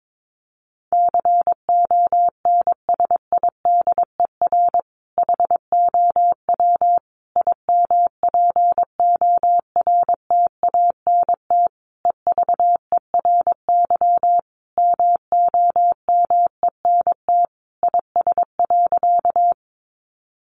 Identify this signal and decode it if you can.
Morse code, heard as a keyed tone, about 22 wpm, 700 Hz. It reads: CONSIDER HOW IMPORTANT EVERY MOMENT IS.
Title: Sentences from Top 1000 Words 22wpm